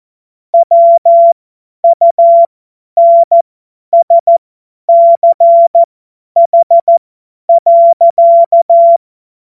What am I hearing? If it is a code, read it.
Morse code, heard as a keyed tone, about 14 wpm, spaced as for 10 wpm, 680 Hz. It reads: WUNSCH.
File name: Witz-01-Juli-2025-50-F70